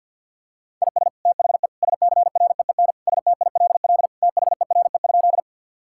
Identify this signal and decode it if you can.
Morse code, heard as a keyed tone, about 52 wpm, 700 Hz. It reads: IS THE SCREEN STILL THERE?
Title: Sentences from Top 1000 Words - Review Part 6 - 50wpm